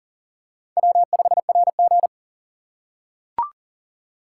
Morse code, heard as a keyed tone, about 40 wpm, 700 Hz. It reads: W5RG
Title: USA Call Signs - 3x Character Spacing - 40wpm